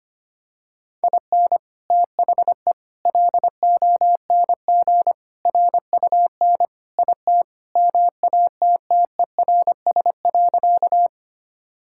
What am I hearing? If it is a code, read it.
Morse code, heard as a keyed tone, about 25 wpm, 700 Hz. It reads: IN THE LONG RUN IT MATTERS.
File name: practice / 25wpm / Sentences with 4-Word Phrases - 25wpm